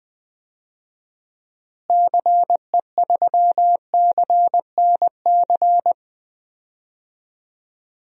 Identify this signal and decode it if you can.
Morse code, heard as a keyed tone, about 20 wpm, 700 Hz. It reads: CE3CNC
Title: DXCC-100 Call Signs - 6 Characters - Repeated 3x - 20wpm